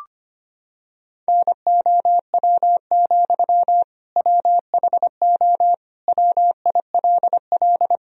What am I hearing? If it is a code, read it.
Morse code, heard as a keyed tone, about 25 wpm, 700 Hz. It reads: NOW, WHO WILL
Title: Sentences from Top 1000 Words - Review Part 1 - 25wpm